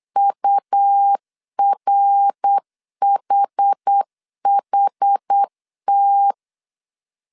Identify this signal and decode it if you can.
Morse code, heard as a keyed tone, about 8 wpm, 800 Hz. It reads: URHHT